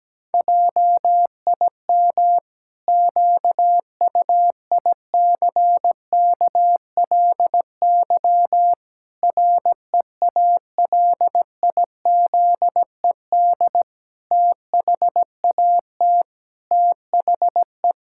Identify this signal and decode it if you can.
Morse code, heard as a keyed tone, about 17 wpm, 700 Hz. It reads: JIM QUICKLY REALIZED THAT THE
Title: Pangram Challenge v2 - 17wpm